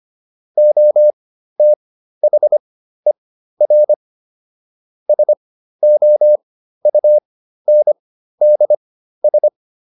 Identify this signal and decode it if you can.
Morse code, heard as a keyed tone, about 25 wpm, spaced as for 13 wpm, 600 Hz. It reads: OTHER SOUNDS